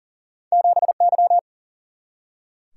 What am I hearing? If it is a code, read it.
Morse code, heard as a keyed tone, about 40 wpm, 700 Hz. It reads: 7 Y
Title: Single-Letter Single-Number 40wpm